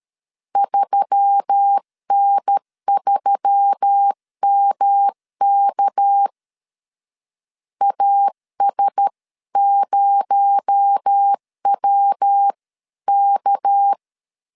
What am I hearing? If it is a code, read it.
Morse code, heard as a keyed tone, about 13 wpm, 800 Hz. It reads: 3N3MK AS0WK
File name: CW / DET / 1 6 1500C S C 12wpm 3533